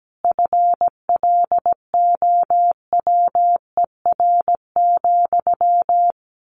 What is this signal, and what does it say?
Morse code, heard as a keyed tone, about 17 wpm, 700 Hz. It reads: FLOWER,
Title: A Sound Of Thunder 17wpm